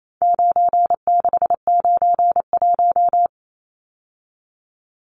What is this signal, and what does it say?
Morse code, heard as a keyed tone, about 28 wpm, 700 Hz. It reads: 9691
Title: Numbers - 4-Digit 28wpm